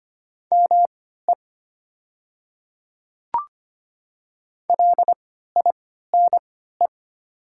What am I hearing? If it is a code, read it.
Morse code, heard as a keyed tone, about 25 wpm, spaced as for 14 wpm, 700 Hz. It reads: ME LINE
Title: 3x Character Spacing - Top 200 Words - 25wpm